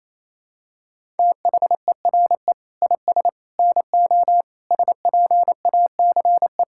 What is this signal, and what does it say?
Morse code, heard as a keyed tone, about 28 wpm, 700 Hz. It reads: THERE IS NO SPACE
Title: Sentences from Top 1000 Words - Review Part 5 - 28wpm